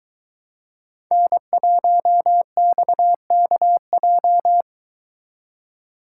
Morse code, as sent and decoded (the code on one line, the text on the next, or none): -. .---- -..- -.- .---
N1XKJ